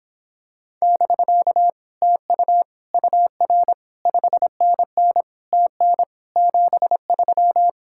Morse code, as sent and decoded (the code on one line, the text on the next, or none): -...-.- / - ..- / ..- .-. / ..... -. -. / - -. / --... ...--
<BK> TU UR 5NN TN 73